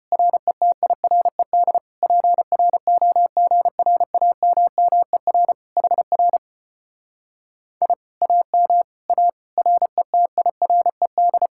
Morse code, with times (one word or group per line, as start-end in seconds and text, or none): 0.1-1.8 RETIRED
2.0-5.5 PROGRAMMER
5.8-6.4 HR
7.8-7.9 I
8.2-8.8 AM
9.1-9.3 A
9.6-11.5 RETIRED